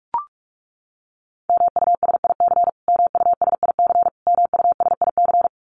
Morse code, heard as a keyed tone, about 44 wpm, 700 Hz. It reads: MUSIC MUSIC MUSIC
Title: Top 500 Words Repeated 45wpm